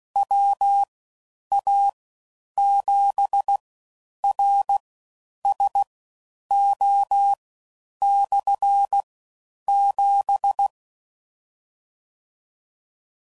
Morse code, heard as a keyed tone, about 16 wpm, spaced as for 9 wpm, 800 Hz. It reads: WA7RSO/7